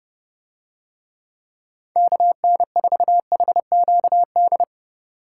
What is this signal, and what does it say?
Morse code, heard as a keyed tone, about 26 wpm, 700 Hz. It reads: KN4HQD